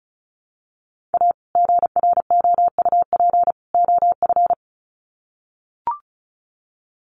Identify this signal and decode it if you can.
Morse code, heard as a keyed tone, about 34 wpm, 700 Hz. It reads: A GROUP OF
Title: Sets of 3 Words 35wpm